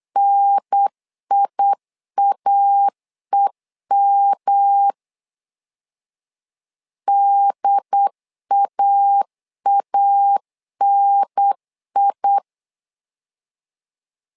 Morse code, heard as a keyed tone, about 8 wpm, 800 Hz. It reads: NIAEM DAANI
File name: LEZIONI / 3 MWRO / 3 M 8wpm 2300